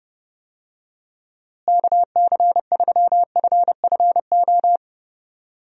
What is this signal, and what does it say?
Morse code, heard as a keyed tone, about 30 wpm, 700 Hz. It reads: KC3FFO
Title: USA Call Signs - 6 Characters - Repeated 3x - 30wpm